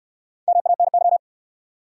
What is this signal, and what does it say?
Morse code, heard as a keyed tone, about 68 wpm, 700 Hz. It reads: MANY